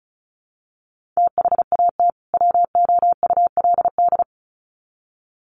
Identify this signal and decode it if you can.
Morse code, heard as a keyed tone, about 34 wpm, 700 Hz. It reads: THAT WOULD
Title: Sets of 2 Words 35wpm